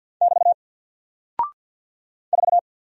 Morse code, heard as a keyed tone, about 50 wpm, 700 Hz. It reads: <BT> 4
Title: Single Character (Letter:Number:Punctuation:Prosign) 50wpm